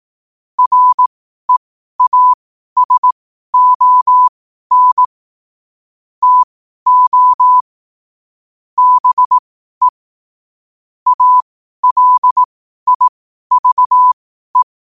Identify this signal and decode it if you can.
Morse code, heard as a keyed tone, about 18 wpm, spaced as for 12 wpm, 1000 Hz. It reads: REASON TO BE ALIVE
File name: qotdfiles / 12wpm